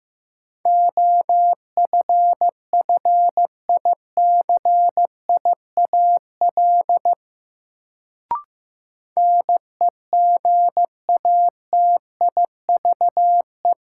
Morse code, heard as a keyed tone, about 15 wpm, 700 Hz. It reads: OFFICIAL NEGATIVE
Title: Sentences from Top 1000 Words - New Word Drill - 15wpm